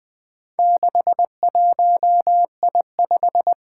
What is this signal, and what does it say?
Morse code, heard as a keyed tone, about 20 wpm, 700 Hz. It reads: 61I5
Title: Four Challenging Letter-Number Combinations 20wpm